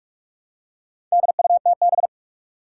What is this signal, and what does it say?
Morse code, heard as a keyed tone, about 46 wpm, 700 Hz. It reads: DUTB
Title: Four Challenging Letter-Number Combinations 45wpm